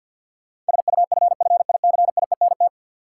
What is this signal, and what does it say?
Morse code, heard as a keyed tone, about 53 wpm, 700 Hz. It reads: SUFFICIENT
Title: Sentences from Top 1400 Words - New Word Drill - 50wpm